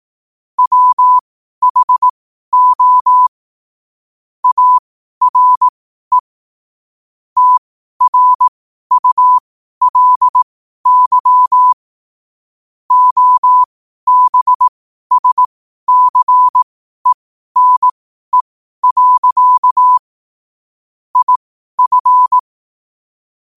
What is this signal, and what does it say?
Morse code, heard as a keyed tone, about 18 wpm, spaced as for 12 wpm, 1000 Hz. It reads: WHO ARE TRULY OBSCENE. IF
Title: 12 WPM morse code quotes for Fri, 01 Mar 2024